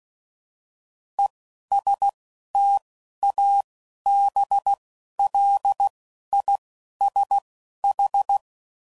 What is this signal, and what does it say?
Morse code, heard as a keyed tone, about 16 wpm, 800 Hz. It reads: E S T A B L I S H